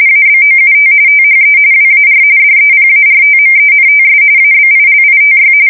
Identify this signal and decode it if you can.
RTTY, about 45.45 baud. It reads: ABOUT THE WAR DEPARTMENTS ABILITY
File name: fsk170 / usnavy1